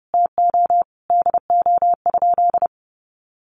Morse code, heard as a keyed tone, about 30 wpm, 700 Hz. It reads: TO DO?
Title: Sentences from Top 1000 Words 30wpm